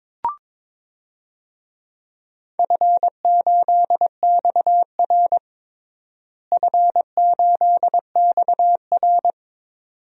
Morse code, heard as a keyed tone, about 22 wpm, 700 Hz. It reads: F8XR F8XR
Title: DXCC-100 Call Signs - 4 Characters - Repeated 2x - 22wpm